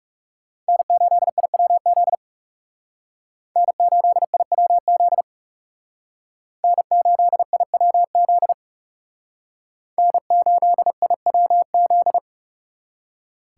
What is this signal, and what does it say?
Morse code, heard as a keyed tone, about 38 wpm, 700 Hz. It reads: N8IWZ N8IWZ N8IWZ N8IWZ